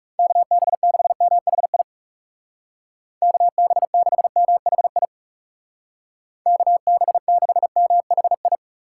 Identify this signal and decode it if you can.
Morse code, heard as a keyed tone, about 36 wpm, 700 Hz. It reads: KB6MHI KB6MHI KB6MHI
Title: Speed Racing - USA 2x3 Call Signs 30wpm